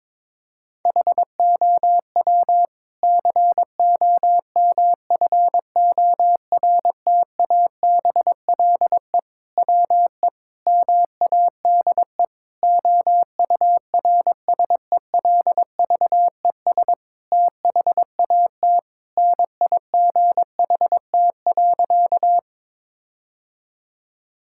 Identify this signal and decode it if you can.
Morse code, heard as a keyed tone, about 22 wpm, 700 Hz. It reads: HOW COMFORTABLE WE MADE OURSELVES THAT NIGHT.
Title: Sentences from Top 1600 Words 22wpm